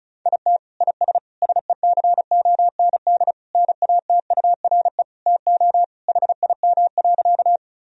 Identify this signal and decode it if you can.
Morse code, heard as a keyed tone, about 34 wpm, 700 Hz. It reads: IT IS SECOND NATURE TO HIM.
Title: Sentences from Top 1000 Words - Review Part 1 - 35wpm